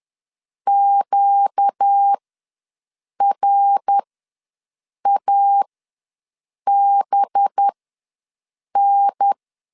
Morse code, heard as a keyed tone, about 10 wpm, spaced as for 6 wpm, 800 Hz. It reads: QRABN
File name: LEZIONI / 5 LBVC / 5 LB ws 10wpm 3616